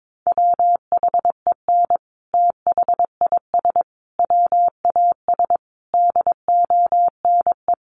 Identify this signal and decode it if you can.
Morse code, heard as a keyed tone, about 22 wpm, 700 Hz. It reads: WHEN THIS WAS DONE
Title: Sentences from Top 900 Words 22wpm